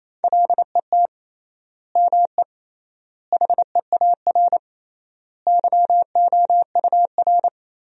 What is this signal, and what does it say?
Morse code, heard as a keyed tone, about 28 wpm, 700 Hz. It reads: LET ME HEAR YOUR
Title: 3x Word-Spacing - Sentences from Top 300 Words - 28wpm